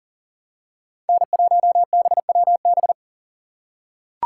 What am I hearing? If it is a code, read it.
Morse code, heard as a keyed tone, about 39 wpm, 700 Hz. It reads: N1BWB
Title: USA Call Signs - 4x Character Spacing - 40wpm